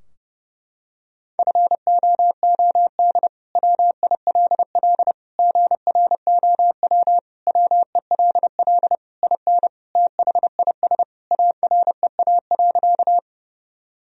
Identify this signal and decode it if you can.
Morse code, heard as a keyed tone, about 32 wpm, 700 Hz. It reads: FOOD WILL GROW WELL IN THIS AREA.
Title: Sentences from Top 1000 Words - Review Part 2 - 30wpm